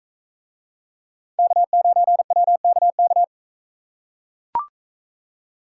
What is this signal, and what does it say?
Morse code, heard as a keyed tone, about 41 wpm, 700 Hz. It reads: K9WKK